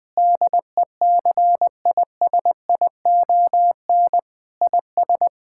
Morse code, heard as a keyed tone, about 20 wpm, 700 Hz. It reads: DECISION IS
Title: Sentences with 8-Word Phrases - 20wpm